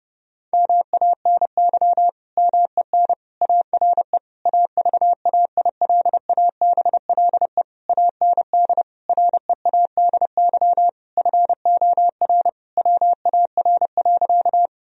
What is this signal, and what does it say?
Morse code, heard as a keyed tone, about 30 wpm, 700 Hz. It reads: MANY MEN ARE AVAILABLE AND READY FOR WAR.
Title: Sentences from Top 1000 Words - Review Part 3 - 30wpm